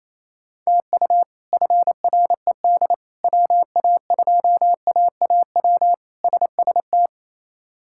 Morse code, heard as a keyed tone, about 27 wpm, 700 Hz. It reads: TU FRED WA2AAW SST
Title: K1USN SST QSO Practice - 28wpm